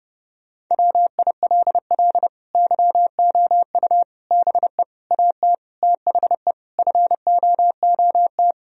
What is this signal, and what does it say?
Morse code, heard as a keyed tone, about 30 wpm, 700 Hz. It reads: WILL YOU BE AT THE FOOT
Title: Sentences from Top 1000 Words - Review Part 3 - 30wpm